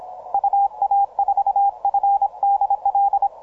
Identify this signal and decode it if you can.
Morse code, heard as a keyed tone, about 26 wpm, 800 Hz, with radio noise. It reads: UA4FDL
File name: simple-file-list / JW LB1QI 17MCW